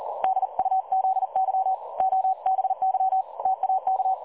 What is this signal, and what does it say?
Morse code, heard as a keyed tone, about 40 wpm, 760 Hz. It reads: DUZY OBY TAK
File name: dlazaawansowanych / nagraniazpasma / websdr recording 3565.0kHz